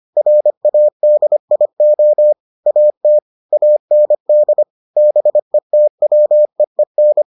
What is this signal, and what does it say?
Morse code, heard as a keyed tone, about 25 wpm, 600 Hz. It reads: RADIO AT AND BETWEEN